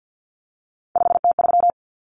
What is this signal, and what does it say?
Morse code, heard as a keyed tone, about 50 wpm, 700 Hz. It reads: 5T3
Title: Common Cut Numbers - 3-Digit 50wpm